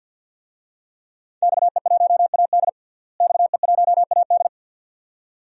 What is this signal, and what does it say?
Morse code, heard as a keyed tone, about 52 wpm, 700 Hz. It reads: XE1AD XE1AD
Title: DXCC-100 Call Signs - 5 Characters - Repeated 2x - 50wpm